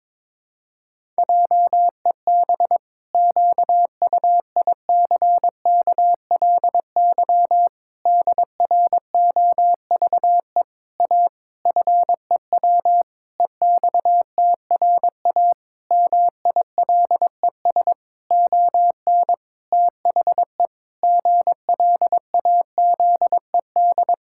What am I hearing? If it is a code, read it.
Morse code, heard as a keyed tone, about 22 wpm, 700 Hz. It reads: JEB QUICKLY DROVE A FEW EXTRA MILES ON THE GLAZED
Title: Pangram Challenge v2 - 22wpm